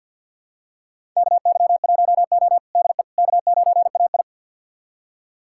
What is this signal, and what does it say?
Morse code, heard as a keyed tone, about 50 wpm, 700 Hz. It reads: KY1O DE K9AI
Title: USA Callsign DE Callsign - 4 Characters - 50wpm